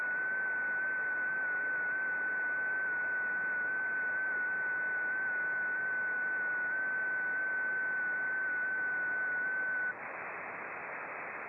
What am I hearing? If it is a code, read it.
FT8.
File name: contests / sept-vhf-k5tr- -0300